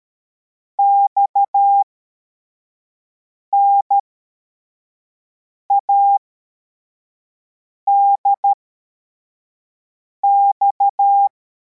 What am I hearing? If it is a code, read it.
Morse code, heard as a keyed tone, about 13 wpm, spaced as for 4 wpm, 800 Hz. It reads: XNADX
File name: formation / cw / Text34